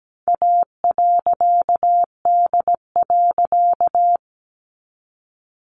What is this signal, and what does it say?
Morse code, heard as a keyed tone, about 17 wpm, 700 Hz. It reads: A.D.
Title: A Sound Of Thunder 17wpm